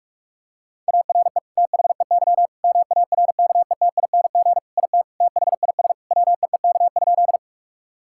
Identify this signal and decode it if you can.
Morse code, heard as a keyed tone, about 41 wpm, 700 Hz. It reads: ARE THEY MARKETING IT THIS WEEK?
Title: Sentences from Top 1000 Words - Review Part 5 - 45wpm